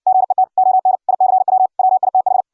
Morse code, heard as a keyed tone, about 60 wpm, 750 Hz. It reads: GET ON ICW WITH